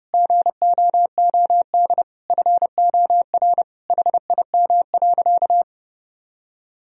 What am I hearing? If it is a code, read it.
Morse code, heard as a keyed tone, about 30 wpm, 700 Hz. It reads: GOOD FOR HIM.
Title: Sentences from Top 1000 Words - Review Part 1 - 30wpm